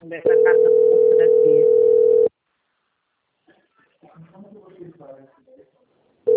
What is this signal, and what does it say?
Ringback tone (North America), 440 + 480 Hz, with speech.
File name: out- - - - - .15441